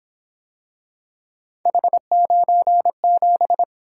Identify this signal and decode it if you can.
Morse code, heard as a keyed tone, about 26 wpm, 700 Hz. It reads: H97
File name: Speed-Racing - Three Letter-Number Combinations - 22wpm